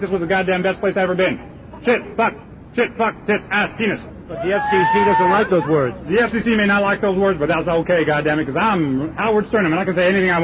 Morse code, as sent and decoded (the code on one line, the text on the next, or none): none